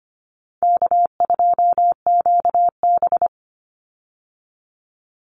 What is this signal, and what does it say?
Morse code, heard as a keyed tone, about 25 wpm, 700 Hz. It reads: K2QB
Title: Easy Call Signs Repeated 2x - 25wpm